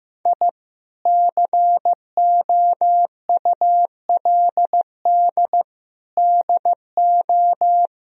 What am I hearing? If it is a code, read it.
Morse code, heard as a keyed tone, about 15 wpm, 700 Hz. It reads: I COULD DO
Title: Sentences from Top 1000 Words - Review Part 3 - 15wpm